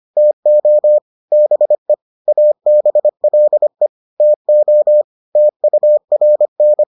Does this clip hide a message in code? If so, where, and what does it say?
Yes: Morse code from 0.2 to 6.8 s, heard as a keyed tone, about 25 wpm, 600 Hz. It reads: TO BE ABLE TO TURN